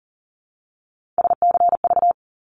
Morse code, heard as a keyed tone, about 40 wpm, 700 Hz. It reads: SCV
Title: ARRL Sections 40wpm